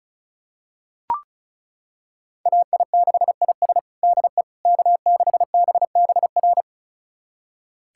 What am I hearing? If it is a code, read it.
Morse code, heard as a keyed tone, about 34 wpm, 690 Hz. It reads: AI6IS DE K6BBR
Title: USA Callsign DE Callsign - 5 Characters - 35wpm